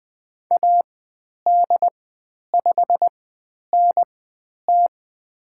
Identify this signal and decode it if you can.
Morse code, heard as a keyed tone, about 20 wpm, spaced as for 10 wpm, 700 Hz. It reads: AD5NT
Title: 095 Lesson 16g - Letter c - Call Signs 20wpm